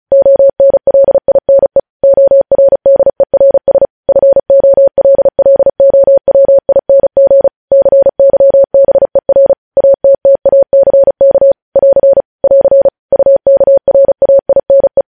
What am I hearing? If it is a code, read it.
Morse code, heard as a keyed tone, about 36 wpm, 550 Hz. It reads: ONLINE ORDERS FOLLOWING CYBER ATTACK <AR> <AR> UKRAINE
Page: News headlines in Morse code at 35 wpm